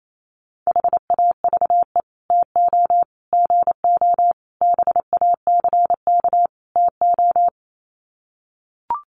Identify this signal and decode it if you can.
Morse code, heard as a keyed tone, about 28 wpm, 700 Hz. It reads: HAVE TO GO BACK TO
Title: Sets of 5 Words Encore2 28wpm